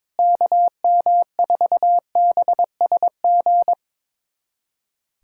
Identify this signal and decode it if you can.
Morse code, heard as a keyed tone, about 21 wpm, 700 Hz. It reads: KM4BSG
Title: USA Callsign DE Callsign - 6 Characters - 22wpm